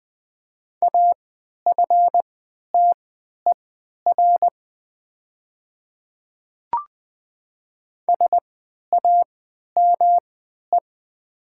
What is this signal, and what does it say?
Morse code, heard as a keyed tone, about 20 wpm, spaced as for 11 wpm, 700 Hz. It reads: AFTER SAME
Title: 3x Character Spacing - Top 200 Words - 20wpm